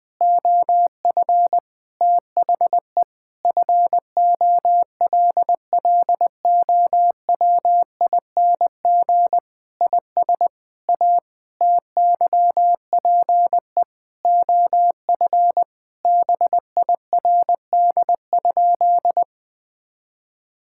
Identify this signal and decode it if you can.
Morse code, heard as a keyed tone, about 20 wpm, 700 Hz. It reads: OF THE FOLLOWING IS A TYPE OF BIRD?
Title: Sentences with 8-Word Phrases - 20wpm